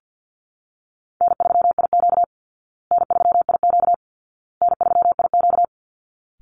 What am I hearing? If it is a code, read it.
Morse code, heard as a keyed tone, about 50 wpm, 700 Hz. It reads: N3IQ N3IQ N3IQ
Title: Call Signs Repeated 2x - 50wpm